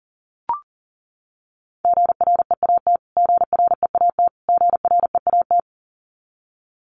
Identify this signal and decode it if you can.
Morse code, heard as a keyed tone, about 40 wpm, 700 Hz. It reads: GREAT GREAT GREAT
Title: Top 200 Words Repeated 40wpm